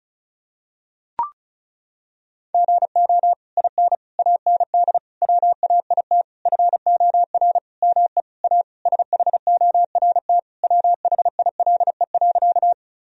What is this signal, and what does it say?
Morse code, heard as a keyed tone, about 36 wpm, 700 Hz. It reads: GO IN AND WAIT FOR ME A SHORT WHILE.
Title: Sentences from Top 1000 Words - Review Part 2 - 35wpm